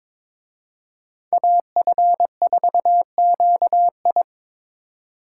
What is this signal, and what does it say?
Morse code, heard as a keyed tone, about 26 wpm, 700 Hz. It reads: AF4QI
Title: Speed Racing - USA Call Signs 22wpm